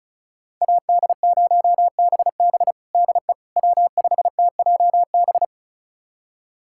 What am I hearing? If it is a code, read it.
Morse code, heard as a keyed tone, about 36 wpm, 700 Hz. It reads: AD0BB DE W5TJB
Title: USA Callsign DE Callsign - 5 Characters - 35wpm